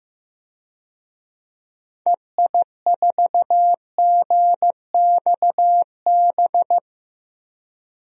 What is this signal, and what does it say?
Morse code, heard as a keyed tone, about 15 wpm, 700 Hz. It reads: EI4GXB
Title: DXCC-100 Call Signs - 6 Characters - 15wpm